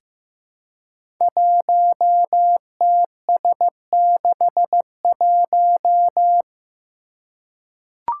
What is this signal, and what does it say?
Morse code, heard as a keyed tone, about 15 wpm, 700 Hz. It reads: 1TS61
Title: Five Challenging Letter-Number Combinations 15wpm